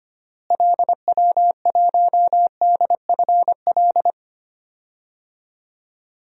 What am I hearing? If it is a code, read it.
Morse code, heard as a keyed tone, about 25 wpm, 700 Hz. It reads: LW1DFL
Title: DXCC-100 Call Signs - 6 Characters - Repeated 3x - 25wpm